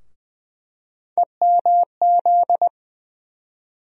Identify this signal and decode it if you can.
Morse code, heard as a keyed tone, about 20 wpm, 700 Hz. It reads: EMZ
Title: USA Call Sign Postfixes - 3 Letters 20wpm